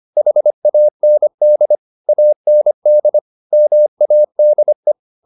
Morse code, heard as a keyed tone, about 25 wpm, 600 Hz. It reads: HAND AND MADE
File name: the old man and the sea part 1 25wpm